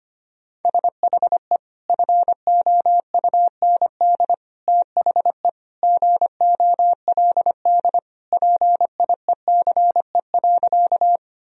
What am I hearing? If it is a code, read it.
Morse code, heard as a keyed tone, about 25 wpm, 700 Hz. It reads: SHE FOUND THE GOLD PIECE.
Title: Sentences from Top 1000 Words - Review Part 3 - 25wpm